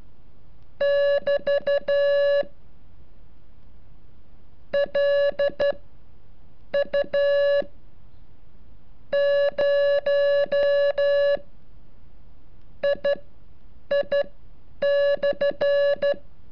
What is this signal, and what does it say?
Morse code, heard as a keyed tone, about 13 wpm, spaced as for 7 wpm, 570 Hz. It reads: <BT> LU 0 II/
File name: curso / ejercicio 35